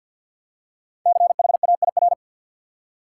Morse code, heard as a keyed tone, about 49 wpm, 700 Hz. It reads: CHAIR
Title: Five Letter Words - 50wpm